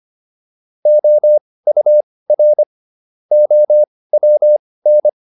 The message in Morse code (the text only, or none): OUR OWN